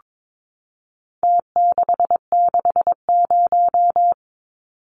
Morse code, heard as a keyed tone, about 22 wpm, 700 Hz. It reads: T660
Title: Numbers - Contest Style - 4-Digit 22wpm